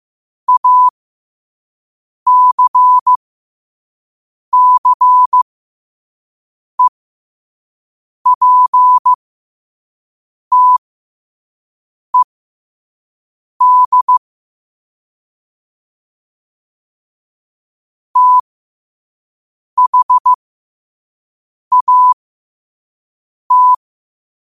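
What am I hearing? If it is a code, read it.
Morse code, heard as a keyed tone, about 15 wpm, spaced as for 5 wpm, 1000 Hz. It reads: ACCEPTED THAT
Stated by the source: Morse code practice at 5 wpm.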